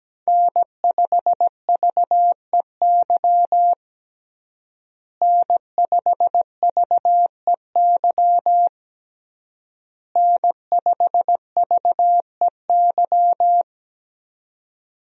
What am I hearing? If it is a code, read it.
Morse code, heard as a keyed tone, about 17 wpm, 700 Hz. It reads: N5VEY N5VEY N5VEY
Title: USA Call Signs - 5 Characters - Repeated 3x - 17wpm